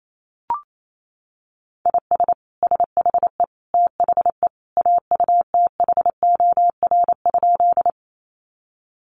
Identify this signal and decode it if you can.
Morse code, heard as a keyed tone, about 28 wpm, 700 Hz. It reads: IS SHE THE AUTHOR?
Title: Sentences from Top 1200 Words 28wpm